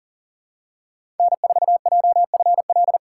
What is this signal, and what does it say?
Morse code, heard as a keyed tone, about 40 wpm, 700 Hz. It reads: N4JFL
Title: USA Call Signs - 5 Characters - 4x Character Spacing - 40wpm